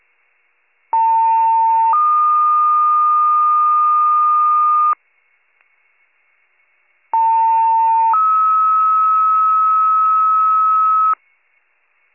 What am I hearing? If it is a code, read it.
Two-tone page, heard as a one-second tone, then a three-second tone, about 900 then 1220 Hz, about 900 then 1290 Hz. It is heard from 0.9 to 11.1 s.